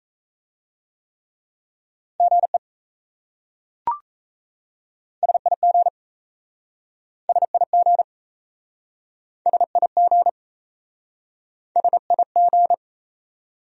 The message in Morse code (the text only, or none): GE SIG SIG SIG SIG